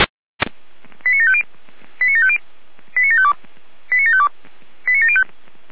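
Five-tone selective call (ZVEI), heard as quick runs of five tones, tones of about 70 ms. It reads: 89750 89750 89752 89752 89705
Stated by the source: ZVEI five-tone selective call received on an SDR.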